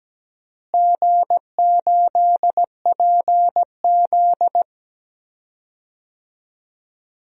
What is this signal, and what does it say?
Morse code, heard as a keyed tone, about 17 wpm, 700 Hz. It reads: G8PZ
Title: DXCC-100 Call Signs - 4 Characters - Repeated 2x - 17wpm